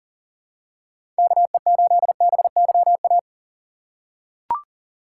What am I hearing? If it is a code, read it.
Morse code, heard as a keyed tone, about 39 wpm, 700 Hz. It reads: KE8BYA
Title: USA Call Signs - 6 Characters - 4x Character Spacing - 40wpm